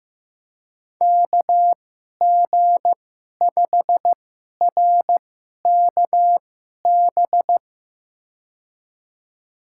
Morse code, heard as a keyed tone, about 15 wpm, spaced as for 11 wpm, 700 Hz. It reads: KG5RKB
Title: USA Call Signs - 6 Characters - 2x Character Spacing - 15wpm